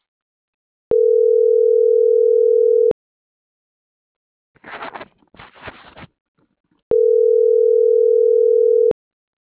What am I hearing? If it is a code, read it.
Ringback tone (North America), 440 + 480 Hz.